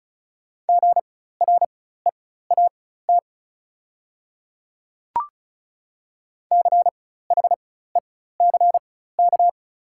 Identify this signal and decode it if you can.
Morse code, heard as a keyed tone, about 36 wpm, spaced as for 16 wpm, 690 Hz. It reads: GREAT CHECK
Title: 4x Character Spacing - Top 500 Words - 35wpm